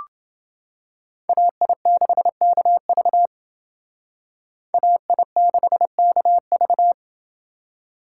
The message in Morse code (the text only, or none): AI6KV AI6KV